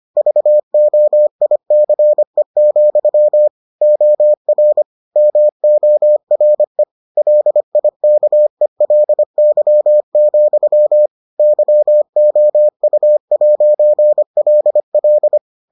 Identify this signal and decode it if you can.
Morse code, heard as a keyed tone, about 25 wpm, 600 Hz. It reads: VOICE, OR MORE LIKELY, YOU'LL